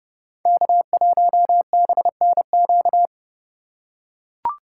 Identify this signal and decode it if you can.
Morse code, heard as a keyed tone, about 30 wpm, 700 Hz. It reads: K1BNQ
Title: USA Call Signs - 5 Characters - 8x Character Spacing - 30wpm